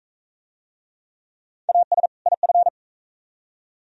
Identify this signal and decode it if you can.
Morse code, heard as a keyed tone, about 41 wpm, 700 Hz. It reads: AS IF